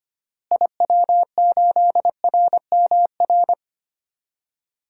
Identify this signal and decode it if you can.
Morse code, heard as a keyed tone, about 25 wpm, 700 Hz. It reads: IW8RMR
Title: DXCC-100 Call Signs - 6 Characters - Repeated 3x - 25wpm